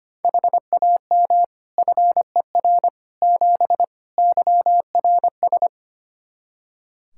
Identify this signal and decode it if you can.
Morse code, heard as a keyed tone, about 25 wpm, 700 Hz. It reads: HAM FER 7 YRS
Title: Practice QSO - Exchange 3 - More Info - 25wpm